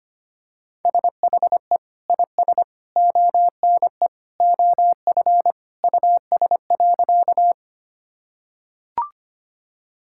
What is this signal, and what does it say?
Morse code, heard as a keyed tone, about 25 wpm, 700 Hz. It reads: SHE IS ONE OF US.
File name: practice / 25wpm / Sentences with 3-Word Phrases - 25wpm